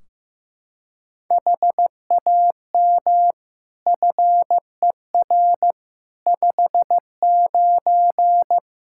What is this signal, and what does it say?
Morse code, heard as a keyed tone, about 16 wpm, 700 Hz. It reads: HAM FER 59
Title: Practice QSO - Exchange 3 - More Info - 15wpm